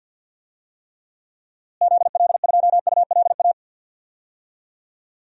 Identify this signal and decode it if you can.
Morse code, heard as a keyed tone, about 47 wpm, 700 Hz. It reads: ZL2ULA
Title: DXCC-100 Call Signs - 6 Characters - Repeated 2x - 50wpm